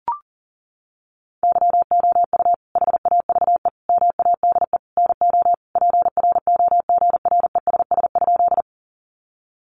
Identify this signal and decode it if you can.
Morse code, heard as a keyed tone, about 40 wpm, 700 Hz. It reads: YOU HAVE MADE NO PROGRESS?
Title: Sentences from Top 1400 Words 40wpm